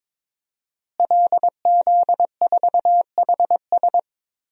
Morse code, heard as a keyed tone, about 22 wpm, 700 Hz. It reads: LZ4HS
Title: DXCC-100 Call Signs - 5 Characters - Repeated 3x - 22wpm